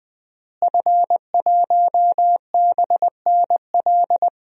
Morse code, heard as a keyed tone, about 20 wpm, 700 Hz. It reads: F1BNL
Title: DXCC-100 Call Signs - 5 Characters - Repeated 2x - 20wpm